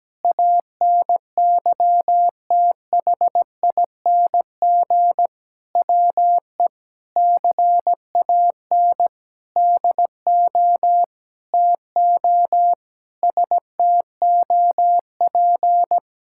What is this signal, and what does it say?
Morse code, heard as a keyed tone, about 17 wpm, 700 Hz. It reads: ANYTHING WE CAN DO TO STOP
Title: Sentences from Top 1000 Words - Review Part 6 - 17wpm